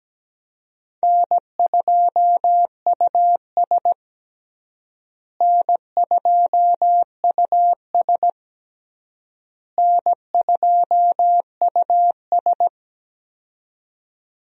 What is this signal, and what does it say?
Morse code, heard as a keyed tone, about 17 wpm, 700 Hz. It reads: N2US N2US N2US
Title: USA Call Signs - 4 Characters - Repeated 3x - 17wpm